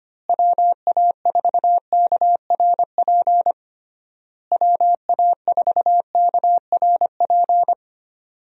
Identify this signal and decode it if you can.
Morse code, heard as a keyed tone, about 25 wpm, 700 Hz. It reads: WA4KRP WA4KRP
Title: USA Call Signs - 6 Characters - Repeated 3x - 25wpm